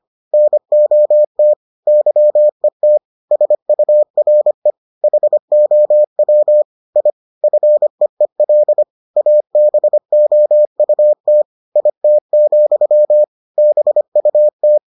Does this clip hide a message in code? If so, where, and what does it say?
Yes: Morse code from 0.3 to 14.8 s, heard as a keyed tone, about 25 wpm, 600 Hz. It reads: NOT YET SURE HOW I FEEL ABOUT IT, BUT